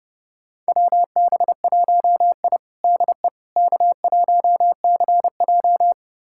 Morse code, heard as a keyed tone, about 30 wpm, 700 Hz. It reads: WB1I DE K1CJ
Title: USA Callsign DE Callsign - 4 Characters - 30wpm